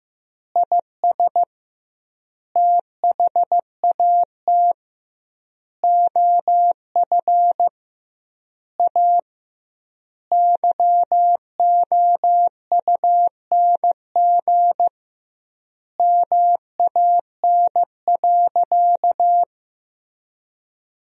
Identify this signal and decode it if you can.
Morse code, heard as a keyed tone, about 15 wpm, 700 Hz. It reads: IS THAT OF A YOUNG MAN.
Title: 2x Word-Spacing - Sentences from Top 500 Words - 15wpm